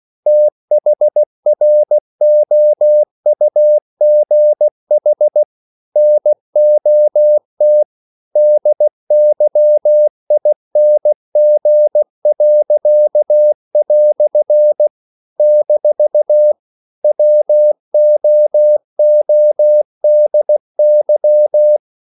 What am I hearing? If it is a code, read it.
Morse code, heard as a keyed tone, about 16 wpm, 600 Hz. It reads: THROUGH NOT DYING." - WOODY
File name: CW / Practice / EnglishQuotations0416